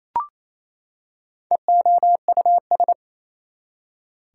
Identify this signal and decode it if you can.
Morse code, heard as a keyed tone, about 28 wpm, 700 Hz. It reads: EOUS
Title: Prefixes and Suffixes 28wpm